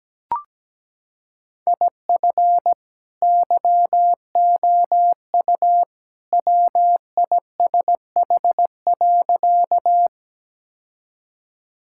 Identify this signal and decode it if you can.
Morse code, heard as a keyed tone, about 17 wpm, 700 Hz. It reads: IF YOU WISH.
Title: Sentences from Top 1600 Words 17wpm